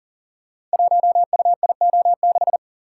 Morse code, heard as a keyed tone, about 39 wpm, 700 Hz. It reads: 1UIO6
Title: Code Groups - 5 Letter:Numbers - 40wpm